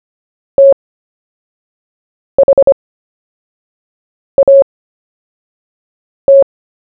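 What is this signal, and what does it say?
Morse code, heard as a keyed tone, about 25 wpm, spaced as for 5 wpm, 550 Hz. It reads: THAT
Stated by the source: CW story-telling practice, Short Story 1 sent at 5 wpm, by VK4JE, Queensland, Australia.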